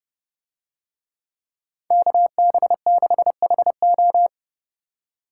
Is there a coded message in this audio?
Yes: Morse code from 1.9 to 4.3 s, heard as a keyed tone, about 29 wpm, 700 Hz. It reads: KB6HO